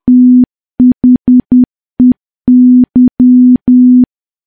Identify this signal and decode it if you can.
Morse code, heard as a keyed tone, about 10 wpm, 250 Hz. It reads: THEY
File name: morse / morse-training-10WPM